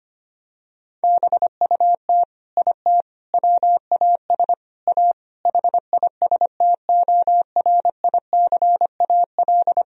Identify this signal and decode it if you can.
Morse code, heard as a keyed tone, about 25 wpm, 700 Hz. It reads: BUT IT WAS A HISTORICAL
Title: Sentences from Top 1000 Words - Review Part 4 - 25wpm